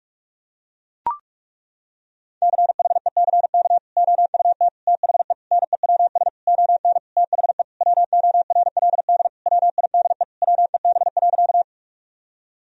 Morse code, heard as a keyed tone, about 43 wpm, 700 Hz. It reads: CHECK OUT THE NEWS ON THE WORLD WIDE WEB.
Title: Sentences from Top 1000 Words - Review Part 4 - 45wpm